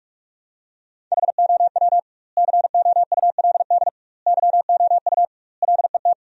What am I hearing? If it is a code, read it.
Morse code, heard as a keyed tone, about 46 wpm, 700 Hz. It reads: HOW COULD YOU LET
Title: 2x Word-Spacing - Sentences from Top 300 Words - 45wpm